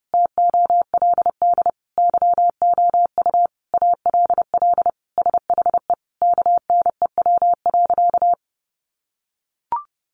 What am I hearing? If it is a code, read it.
Morse code, heard as a keyed tone, about 30 wpm, 700 Hz. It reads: TOLD YOU ALL SHE KNEW.
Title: Sentences from Top 1300 Words 30wpm